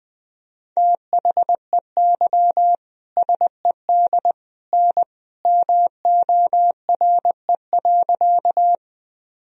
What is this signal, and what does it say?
Morse code, heard as a keyed tone, about 21 wpm, 700 Hz. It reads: THEY SED N MORE.
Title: Sentences from Top 100 Words and Abbreviations 20wpm